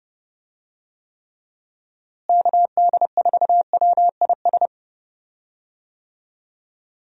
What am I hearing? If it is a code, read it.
Morse code, heard as a keyed tone, about 30 wpm, 700 Hz. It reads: KD4WIS
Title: USA Call Signs - 6 Characters - Repeated 2x - 30wpm